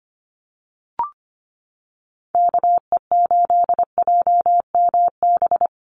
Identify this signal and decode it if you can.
Morse code, heard as a keyed tone, about 25 wpm, 700 Hz. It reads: KE8JMB
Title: USA Call Signs Repeated 3x 25wpm